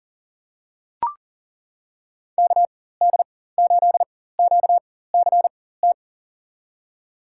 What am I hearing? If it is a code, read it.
Morse code, heard as a keyed tone, about 40 wpm, spaced as for 19 wpm, 690 Hz. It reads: KD8QCT
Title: USA Call Signs - 4x Character Spacing - 40wpm